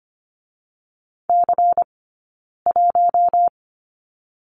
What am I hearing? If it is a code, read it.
Morse code, heard as a keyed tone, about 25 wpm, 700 Hz. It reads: C 1
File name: course / 25:10wpm / 092 Lesson 16d - Letter c - Two Characters 25wpm